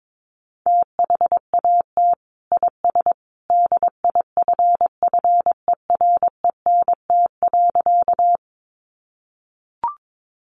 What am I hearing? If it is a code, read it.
Morse code, heard as a keyed tone, about 22 wpm, 700 Hz. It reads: THAT IS DIFFERENT.
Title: Sentences from Top 600 Words 22wpm